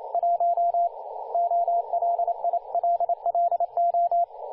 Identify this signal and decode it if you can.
Morse code, heard as a keyed tone, about 29 wpm, 690 Hz, with radio noise. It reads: 1 OLILLO
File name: du1-ja1pbv-1-28-040mhz-cw-cqing